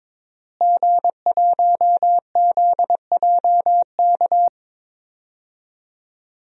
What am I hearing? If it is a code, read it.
Morse code, heard as a keyed tone, about 22 wpm, 700 Hz. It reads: G1ZJK
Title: DXCC-100 Call Signs - 5 Characters - Repeated 3x - 22wpm